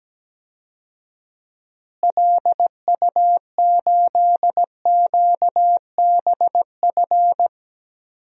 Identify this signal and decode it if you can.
Morse code, heard as a keyed tone, about 17 wpm, 700 Hz. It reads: LU8QBF
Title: DXCC-100 Call Signs - 6 Characters - Repeated 2x - 17wpm